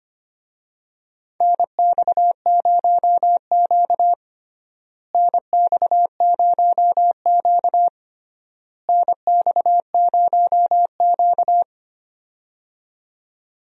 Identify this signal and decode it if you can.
Morse code, heard as a keyed tone, about 25 wpm, 700 Hz. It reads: NX0Q NX0Q NX0Q
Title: USA Call Signs - 4 Characters - Repeated 3x - 25wpm